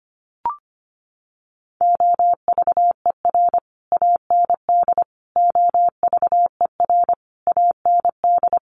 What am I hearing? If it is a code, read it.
Morse code, heard as a keyed tone, about 25 wpm, 700 Hz. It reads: OVER AND OVER AND
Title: Sets of 5 Words Encore 25wpm